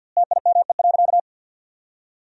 Morse code, heard as a keyed tone, about 50 wpm, 700 Hz. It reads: TIME.
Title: 2x Word-Spacing - Sentences from Top 200 Words - 50wpm